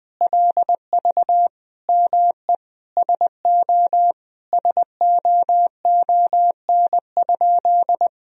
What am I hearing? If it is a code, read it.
Morse code, heard as a keyed tone, about 21 wpm, 700 Hz. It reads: LV ME SO SOON?
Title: Sentences from Top 400 Words and Abbreviations 20wpm